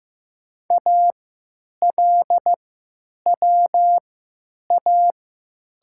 Morse code, heard as a keyed tone, about 15 wpm, spaced as for 9 wpm, 700 Hz. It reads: ALWA
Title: 3x Character Spacing - Top 500 Words - 15wpm